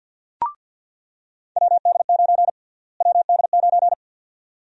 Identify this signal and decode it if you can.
Morse code, heard as a keyed tone, about 46 wpm, 700 Hz. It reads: WD9 WD9
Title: USA Call Sign Prefixes - Repeated 2x - 50wpm